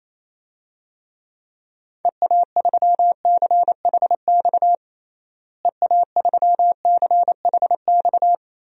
Morse code, heard as a keyed tone, about 28 wpm, 700 Hz. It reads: EA3CHX EA3CHX
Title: DXCC-100 Call Signs - 6 Characters - Repeated 2x - 28wpm